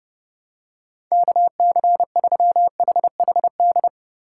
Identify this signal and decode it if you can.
Morse code, heard as a keyed tone, about 32 wpm, 700 Hz. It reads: KC3HHD
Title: USA Call Signs - 6 Characters - 2x Character Spacing - 30wpm